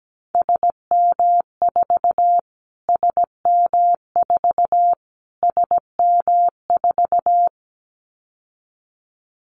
Morse code, heard as a keyed tone, about 17 wpm, 700 Hz. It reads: SM4 SM4 SM4
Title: Callsign Prefixes for Top 25 DXCC - Repeated 3x - 17wpm